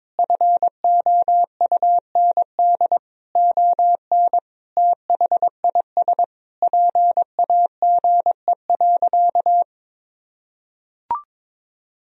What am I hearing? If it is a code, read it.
Morse code, heard as a keyed tone, about 22 wpm, 700 Hz. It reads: FOUND ON THIS PAGE.
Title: Sentences from Top 1000 Words - Review Part 1 - 22wpm